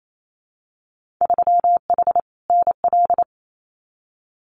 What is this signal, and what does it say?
Morse code, heard as a keyed tone, about 28 wpm, 700 Hz. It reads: 3H NL
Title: Winter Field Day Exchanges Encore 28wpm